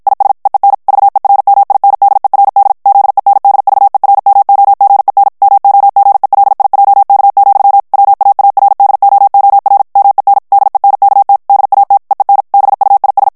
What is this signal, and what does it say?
Morse code, heard as a keyed tone, about 56 wpm, 800 Hz. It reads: IS EEN VERMINDERD ZENDVERMOGEN MOGELIJK, WAARDOOR MEN DENKT DAT EEN BUIS
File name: morse mp3 / HOLLANDAIS texte clair 2 vitesse 50 mots minute